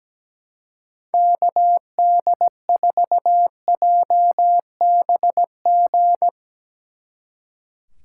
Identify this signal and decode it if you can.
Morse code, heard as a keyed tone, about 18 wpm, 700 Hz. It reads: KD4JBG
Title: Speed Racing - USA 2x3 Call Signs 17wpm